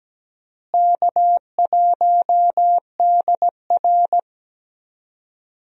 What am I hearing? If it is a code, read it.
Morse code, heard as a keyed tone, about 17 wpm, 700 Hz. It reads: K1DR
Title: USA Call Signs - 4 Characters - Repeated 2x - 17wpm